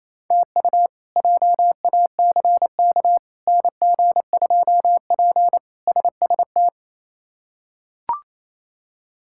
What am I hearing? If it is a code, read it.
Morse code, heard as a keyed tone, about 28 wpm, 700 Hz. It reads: TU JACK NG2P SST
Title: K1USN SST QSO Practice - 28wpm